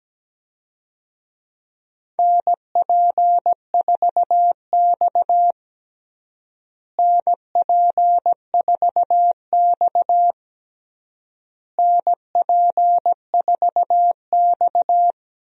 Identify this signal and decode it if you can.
Morse code, heard as a keyed tone, about 17 wpm, 700 Hz. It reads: NP4X NP4X NP4X
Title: USA Call Signs - 4 Characters - Repeated 3x - 17wpm